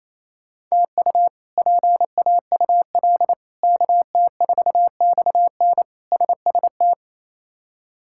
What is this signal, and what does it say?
Morse code, heard as a keyed tone, about 28 wpm, 700 Hz. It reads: TU PAUL KT4XN SST
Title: K1USN SST QSO Practice - 28wpm